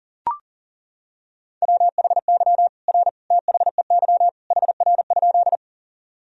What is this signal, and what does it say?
Morse code, heard as a keyed tone, about 40 wpm, 700 Hz. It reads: WHY R THEY HR?
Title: Sentences from Top 200 Words and Abbreviations 40wpm